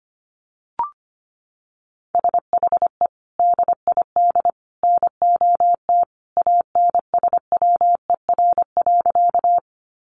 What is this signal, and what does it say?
Morse code, heard as a keyed tone, about 25 wpm, 700 Hz. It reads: SHE DID NOT ANSWER.
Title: Sentences from Top 600 Words 25wpm